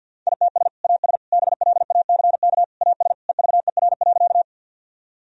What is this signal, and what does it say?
Morse code, heard as a keyed tone, about 53 wpm, 700 Hz. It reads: ITS AS BLACK AS EVER.
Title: Sentences from Top 1000 Words - Review Part 2 - 50wpm